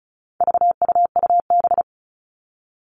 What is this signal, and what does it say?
Morse code, heard as a keyed tone, about 34 wpm, 700 Hz. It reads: VUUB
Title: All Cut Numbers - 4-Digit 35wpm